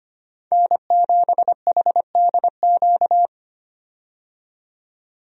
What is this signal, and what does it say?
Morse code, heard as a keyed tone, about 25 wpm, 700 Hz. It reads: N7HDQ
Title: USA Call Signs - 5 Characters - 25wpm